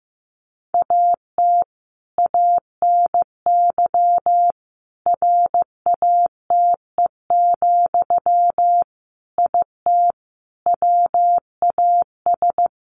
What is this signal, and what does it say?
Morse code, heard as a keyed tone, about 15 wpm, 700 Hz. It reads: AT ANY RATE, IT WAS
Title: Sentences from Top 1200 Words 15wpm